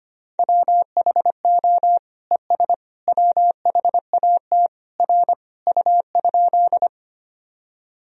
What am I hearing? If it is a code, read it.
Morse code, heard as a keyed tone, about 25 wpm, 700 Hz. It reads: WHO ES WHAT R U?
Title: Sentences from Top 100 Words and Abbreviations 25wpm